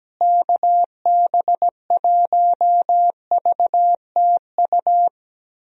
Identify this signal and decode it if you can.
Morse code, heard as a keyed tone, about 17 wpm, 700 Hz. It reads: KB1VTU
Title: USA Call Signs - 6 Characters - Repeated 3x - 17wpm